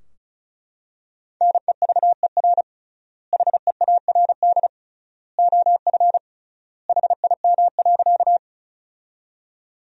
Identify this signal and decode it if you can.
Morse code, heard as a keyed tone, about 34 wpm, 700 Hz. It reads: NEVER HEARD OF HIM.
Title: 3x Word-Spacing - Sentences from Top 400 Words - 35wpm